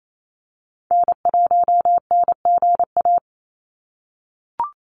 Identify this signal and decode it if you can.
Morse code, heard as a keyed tone, about 28 wpm, 700 Hz. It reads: N1NGA
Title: USA Call Signs 28wpm